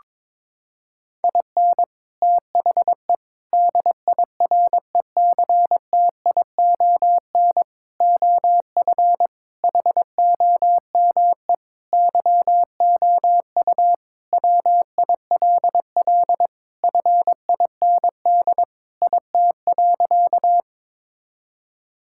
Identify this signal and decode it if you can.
Morse code, heard as a keyed tone, about 22 wpm, 700 Hz. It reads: IN THE DIRECTION OF HOME YOU WILL FIND IT.
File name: practice / 22wpm / Sentences with 4-Word Phrases - 22wpm